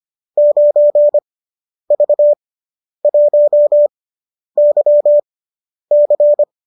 Morse code, heard as a keyed tone, about 25 wpm, spaced as for 10 wpm, 570 Hz. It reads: 9V1YC